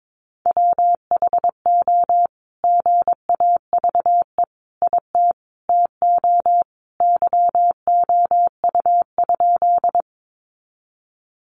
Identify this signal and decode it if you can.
Morse code, heard as a keyed tone, about 22 wpm, 700 Hz. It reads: WHO GAVE IT TO YOU?
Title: Sentences from Top 500 Words 22wpm